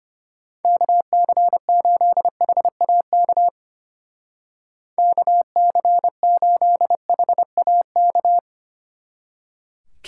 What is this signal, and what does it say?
Morse code, heard as a keyed tone, about 31 wpm, 700 Hz. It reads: KC8HAK KC8HAK
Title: Speed Racing - USA Call Signs 25wpm